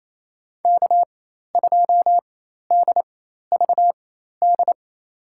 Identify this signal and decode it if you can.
Morse code, heard as a keyed tone, about 28 wpm, spaced as for 13 wpm, 700 Hz. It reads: K2DVD
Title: USA Call Signs - 5 Characters - 4x Character Spacing - 28wpm